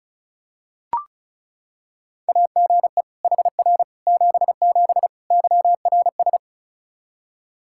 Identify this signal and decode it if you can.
Morse code, heard as a keyed tone, about 34 wpm, 700 Hz. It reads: AGE HR 77 YRS
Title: Practice QSO - Exchange 3 - More Info - 35wpm